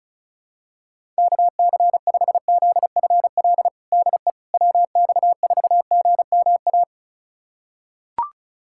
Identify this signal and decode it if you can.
Morse code, heard as a keyed tone, about 34 wpm, 690 Hz. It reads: KC5ZFL DE WX4GMA
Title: USA Callsign DE Callsign - 6 Characters - 35wpm